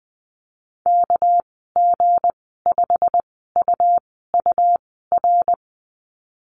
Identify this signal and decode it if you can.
Morse code, heard as a keyed tone, about 20 wpm, 700 Hz. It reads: K G 5 U U R
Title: 231 Lesson 37g - Letter x - Call Signs 20wpm